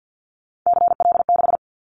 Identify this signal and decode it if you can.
Morse code, heard as a keyed tone, about 50 wpm, 700 Hz. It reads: CL6
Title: Callsign Prefixes for Top 100 DXCC - 50wpm